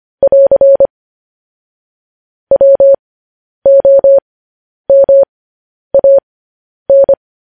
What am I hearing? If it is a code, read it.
Morse code, heard as a keyed tone, about 25 wpm, spaced as for 10 wpm, 550 Hz. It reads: <AR> WOMAN
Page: news headlines in morse code at 10 wpm